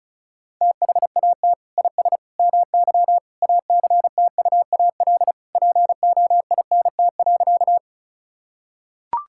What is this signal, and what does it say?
Morse code, heard as a keyed tone, about 34 wpm, 700 Hz. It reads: THAT IS MY ACTUAL POINT.
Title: Sentences from Top 1000 Words - Review Part 5 - 35wpm